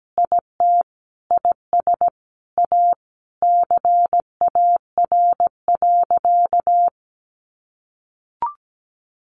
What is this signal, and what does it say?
Morse code, heard as a keyed tone, about 17 wpm, 700 Hz. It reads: IT IS A CAR.
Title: Easy Phrases 17wpm